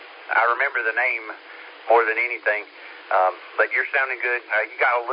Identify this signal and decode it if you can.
Emergency Alert System (SAME) tones, heard as harsh buzzing bursts, with speech. It is heard from 0.6 to 4.1 s.